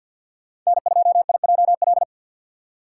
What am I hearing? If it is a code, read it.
Morse code, heard as a keyed tone, about 51 wpm, 700 Hz. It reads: N2IJL
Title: USA Call Signs - 5 Characters - 2x Character Spacing - 50wpm